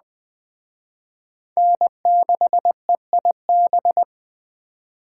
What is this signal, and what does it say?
Morse code, heard as a keyed tone, about 20 wpm, 700 Hz. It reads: N6EIB
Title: Speed Racing - USA 1x3 Call Signs 17wpm